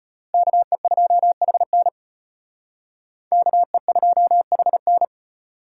Morse code, heard as a keyed tone, about 32 wpm, 700 Hz. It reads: KE2HN KE2HN